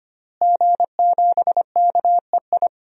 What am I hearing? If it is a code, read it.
Morse code, heard as a keyed tone, about 25 wpm, 700 Hz. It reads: G7KEI
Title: DXCC-100 Call Signs - 5 Characters - Repeated 3x - 25wpm